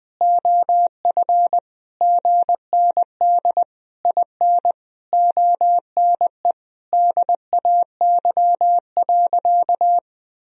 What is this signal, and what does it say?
Morse code, heard as a keyed tone, about 20 wpm, 700 Hz. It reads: OF GND IN ONE DAY.
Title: Sentences from Top 500 Words and Abbreviations 20wpm